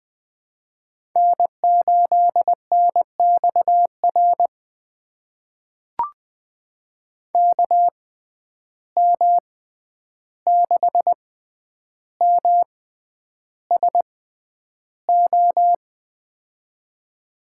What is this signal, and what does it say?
Morse code, heard as a keyed tone, about 20 wpm, spaced as for 10 wpm, 700 Hz. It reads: N8NXR KM6MSO
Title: USA Call Signs - 6x Character Spacing - 20wpm